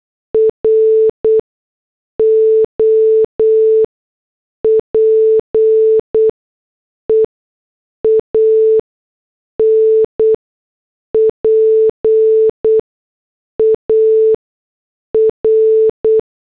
Morse code, heard as a keyed tone, about 8 wpm, spaced as for 6 wpm, 440 Hz. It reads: ROPEAN PAR
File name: morse / headlines